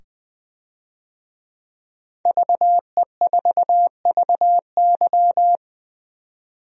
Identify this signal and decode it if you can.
Morse code, heard as a keyed tone, about 21 wpm, 700 Hz. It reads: VE4VY
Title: DXCC-100 Call Signs - 5 Characters - 20wpm